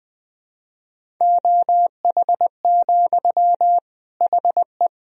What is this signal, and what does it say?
Morse code, heard as a keyed tone, about 20 wpm, 700 Hz. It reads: OH, HE
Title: Sentences from Top 500 Words and Abbreviations 20wpm